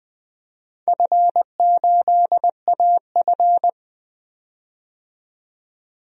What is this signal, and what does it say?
Morse code, heard as a keyed tone, about 20 wpm, 700 Hz. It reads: F8AF
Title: DXCC-100 Call Signs - 4 Characters - 20wpm